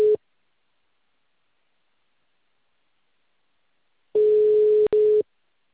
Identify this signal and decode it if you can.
Ringback tone (Europe, 425 Hz).
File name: out- -2044- - - .1607